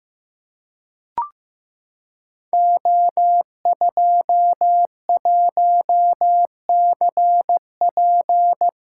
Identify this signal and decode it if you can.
Morse code, heard as a keyed tone, about 15 wpm, 700 Hz. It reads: O21CP
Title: Code Groups - 5 Letter:Numbers - 15wpm